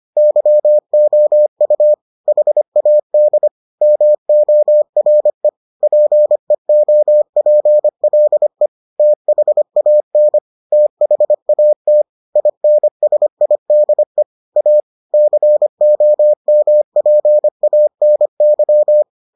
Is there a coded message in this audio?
Yes: Morse code from 0.2 to 19.0 s, heard as a keyed tone, about 25 wpm, 600 Hz. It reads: YOU HAD MORE PEOPLE THAN THAT INSIDE A COMPANY